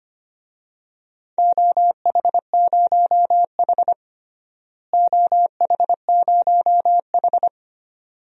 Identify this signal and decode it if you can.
Morse code, heard as a keyed tone, about 25 wpm, 700 Hz. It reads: OH0H OH0H
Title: DXCC-100 Call Signs - 4 Characters - Repeated 2x - 25wpm